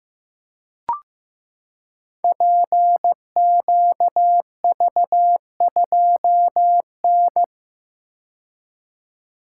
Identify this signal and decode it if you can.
Morse code, heard as a keyed tone, about 16 wpm, 700 Hz. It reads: PQV2N
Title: Code Groups - 5 Letter:Numbers - 15wpm